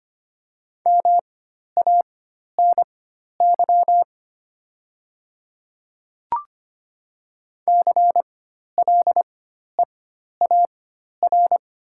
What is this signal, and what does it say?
Morse code, heard as a keyed tone, about 25 wpm, spaced as for 12 wpm, 690 Hz. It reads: MANY CLEAR
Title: 4x Character Spacing - Top 500 Words - 25wpm